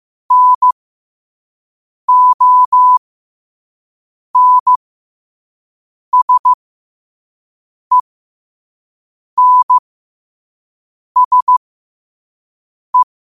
Morse code, heard as a keyed tone, about 15 wpm, spaced as for 5 wpm, 1000 Hz. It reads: NONSENSE
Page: New quotes every day in morse code at 5 Words per minute.